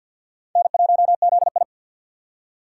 Morse code, heard as a keyed tone, about 49 wpm, 700 Hz. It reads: N1ZI
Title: USA Call Signs - 4x Character Spacing - 50wpm